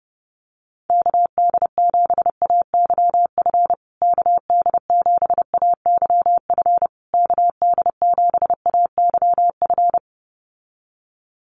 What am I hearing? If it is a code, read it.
Morse code, heard as a keyed tone, about 30 wpm, 700 Hz. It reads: KD7AYF KD7AYF KD7AYF
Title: USA Call Signs Repeated 3x 30wpm